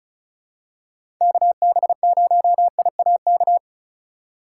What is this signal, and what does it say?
Morse code, heard as a keyed tone, about 36 wpm, 700 Hz. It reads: KB0IAK
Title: USA Call Signs - 6 Characters - 2x Character Spacing - 35wpm